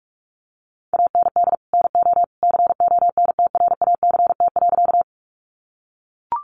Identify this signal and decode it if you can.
Morse code, heard as a keyed tone, about 44 wpm, 700 Hz. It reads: AND NO CONTRACT.
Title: Sentences from Top 1400 Words 45wpm